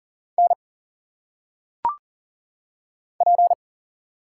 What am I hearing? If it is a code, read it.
Morse code, heard as a keyed tone, about 40 wpm, 700 Hz. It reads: N P
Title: Single Letters - Phonetic Pronunciation 40wpm